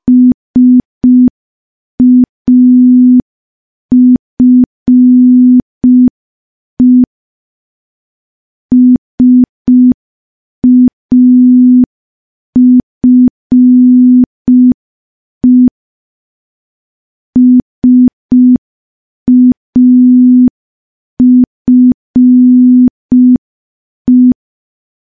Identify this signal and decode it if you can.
Morse code, heard as a keyed tone, about 5 wpm, 250 Hz. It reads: SAFE SAFE SAFE
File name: morse / morse-training-05WPM